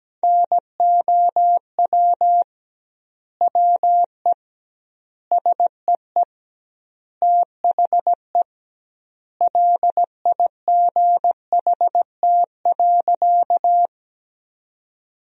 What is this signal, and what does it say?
Morse code, heard as a keyed tone, about 17 wpm, 700 Hz. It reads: NOW WE SEE THE LIGHT.
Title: 2x Word-Spacing - Sentences from Top 200 Words - 17wpm